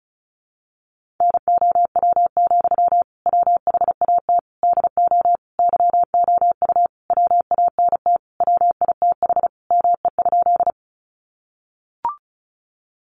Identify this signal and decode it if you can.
Morse code, heard as a keyed tone, about 34 wpm, 700 Hz. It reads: NOW, WHAT DO YOU WANT WITH ME?
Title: Sentences from Top 200 Words 35wpm